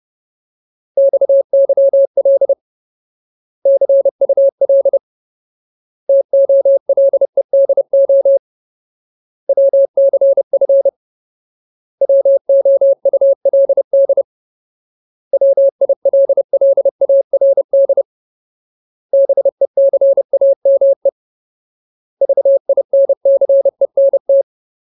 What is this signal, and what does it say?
Morse code, heard as a keyed tone, about 30 wpm, 550 Hz. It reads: XYL CUL TOLEDO WCF WOULD WILLARD BECAME VINCENT
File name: Ultimate Word List / 30WPM Ultimate